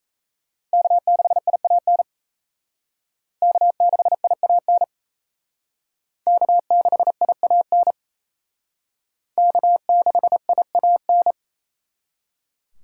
Morse code, heard as a keyed tone, about 36 wpm, 700 Hz. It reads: K6IAN K6IAN K6IAN K6IAN